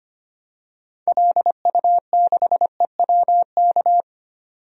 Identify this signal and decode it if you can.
Morse code, heard as a keyed tone, about 25 wpm, 700 Hz. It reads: LU6EWK
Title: DXCC-100 Call Signs - 6 Characters - 25wpm